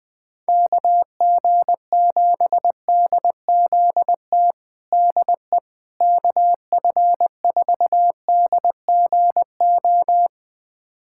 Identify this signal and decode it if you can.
Morse code, heard as a keyed tone, about 20 wpm, 700 Hz. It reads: KG7DZT DE KF4DGO
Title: USA Callsign DE Callsign - 6 Characters - 20wpm